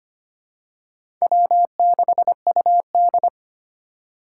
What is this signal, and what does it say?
Morse code, heard as a keyed tone, about 25 wpm, 700 Hz. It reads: W6UD
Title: USA Call Signs - 4 Characters - 2x Character Spacing - 25wpm